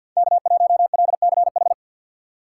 Morse code, heard as a keyed tone, about 64 wpm, 700 Hz. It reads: K1LCH